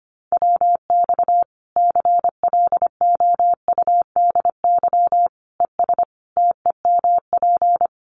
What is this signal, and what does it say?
Morse code, heard as a keyed tone, about 25 wpm, 700 Hz. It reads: WX CLOUDY ES TEMP
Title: Common QSO Phrases 25wpm